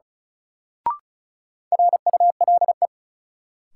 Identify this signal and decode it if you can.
Morse code, heard as a keyed tone, about 36 wpm, 700 Hz. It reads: RULE
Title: Sentences from Top 400 Words - New Word Drill - 35wpm